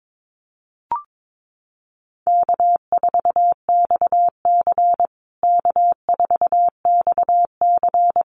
Morse code, heard as a keyed tone, about 22 wpm, 700 Hz. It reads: K4XC K4XC
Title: USA Call Signs Repeated 3x 22wpm